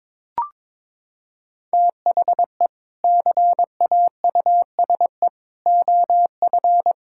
Morse code, heard as a keyed tone, about 22 wpm, 700 Hz. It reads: THE CAUSE OF
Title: Sentences from Top 1600 Words 22wpm